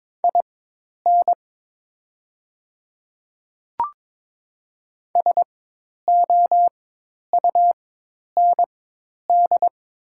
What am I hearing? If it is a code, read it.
Morse code, heard as a keyed tone, about 22 wpm, spaced as for 10 wpm, 700 Hz. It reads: IN SOUND
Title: 4x Character Spacing - Top 300 Words - 22wpm